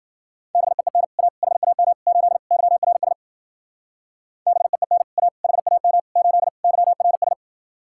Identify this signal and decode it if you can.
Morse code, heard as a keyed tone, about 52 wpm, 700 Hz. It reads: BEEN A HAM 8 YRS BEEN A HAM 8 YRS